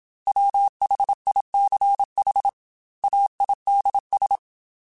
Morse code, heard as a keyed tone, about 27 wpm, 800 Hz. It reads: WHICH AIDS